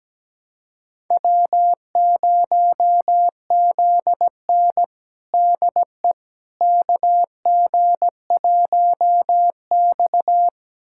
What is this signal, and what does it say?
Morse code, heard as a keyed tone, about 17 wpm, 700 Hz. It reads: W0ZN DE KG1X
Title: USA Callsign DE Callsign - 4 Characters - 17wpm